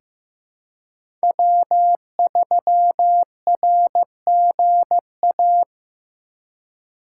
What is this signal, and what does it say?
Morse code, heard as a keyed tone, about 15 wpm, 700 Hz. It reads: W3RGA
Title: K1USN SST QSO Practice - 15wpm